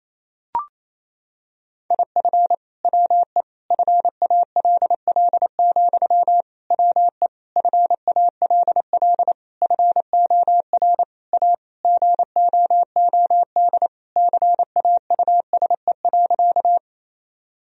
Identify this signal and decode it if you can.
Morse code, heard as a keyed tone, about 27 wpm, 700 Hz. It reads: IF WE FALL, WE FALL FOR A GOOD CAUSE.
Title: Sentences from Top 1000 Words - Review Part 2 - 28wpm